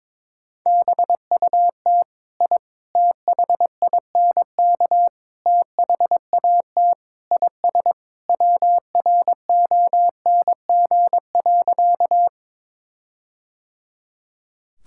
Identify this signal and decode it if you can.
Morse code, heard as a keyed tone, about 22 wpm, 700 Hz. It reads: BUT I THINK THAT IS WRONG.
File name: Sentences with 4-Word Phrases - 22wpm